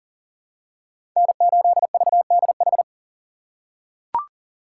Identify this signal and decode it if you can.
Morse code, heard as a keyed tone, about 37 wpm, 700 Hz. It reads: N8VDH
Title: USA Call Signs - 5 Characters - 3x Character Spacing - 40wpm